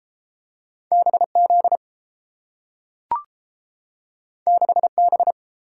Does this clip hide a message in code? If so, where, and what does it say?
Yes: Morse code from 0.9 to 5.3 s, heard as a keyed tone, about 30 wpm, 700 Hz. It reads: BZ 6B